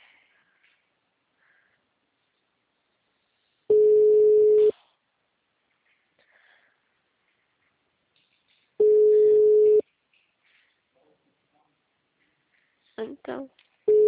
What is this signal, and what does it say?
Ringback tone (Europe, 425 Hz), with speech.